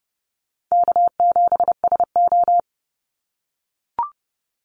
Morse code, heard as a keyed tone, about 30 wpm, 700 Hz. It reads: K7SO
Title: NA SOTA Activator Callsigns 30wpm